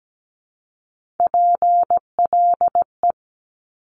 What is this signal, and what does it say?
Morse code, heard as a keyed tone, about 17 wpm, 700 Hz. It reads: PLE
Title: Top Three Letter Combinations in Words 17wpm